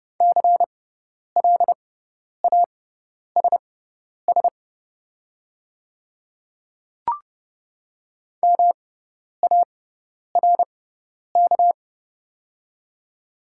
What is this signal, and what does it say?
Morse code, heard as a keyed tone, about 30 wpm, spaced as for 10 wpm, 690 Hz. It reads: CLASS MARK
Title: 6x Character Spacing - Top 500 Words - 30wpm